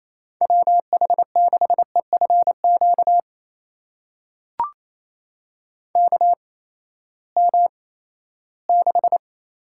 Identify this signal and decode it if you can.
Morse code, heard as a keyed tone, about 28 wpm, spaced as for 16 wpm, 700 Hz. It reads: WH6EFQ KM6
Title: USA Call Signs - 6 Characters - 8x Character Spacing - 28wpm